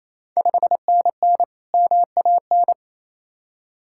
Morse code, heard as a keyed tone, about 28 wpm, 700 Hz. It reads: 5NN MAN
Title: Ontario QSO Party Exchange 28wpm